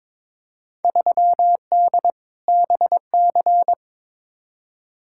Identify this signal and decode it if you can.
Morse code, heard as a keyed tone, about 22 wpm, 700 Hz. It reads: 3D BC
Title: Field Day Exchanges Encore - 22wpm